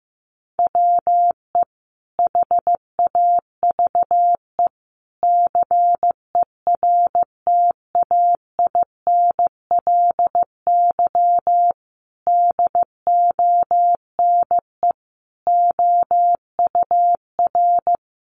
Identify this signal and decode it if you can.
Morse code, heard as a keyed tone, about 15 wpm, 700 Hz. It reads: WE HAVE CERTAINLY DONE OUR
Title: Sentences from Top 900 Words 15wpm